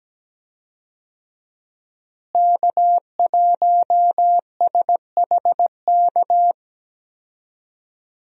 Morse code, heard as a keyed tone, about 17 wpm, 700 Hz. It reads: K1SHK
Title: USA Call Signs - 5 Characters - Repeated 2x - 17wpm